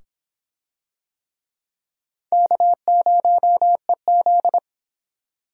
Morse code, heard as a keyed tone, about 23 wpm, 700 Hz. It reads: K0EZ